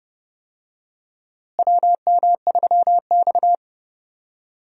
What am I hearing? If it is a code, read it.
Morse code, heard as a keyed tone, about 30 wpm, 700 Hz. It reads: WM3X
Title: USA Call Signs - 4 Characters - Repeated 3x - 30wpm